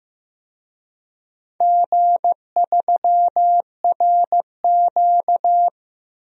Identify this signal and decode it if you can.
Morse code, heard as a keyed tone, about 15 wpm, 700 Hz. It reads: G3RQ
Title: DXCC-100 Call Signs - 4 Characters - 15wpm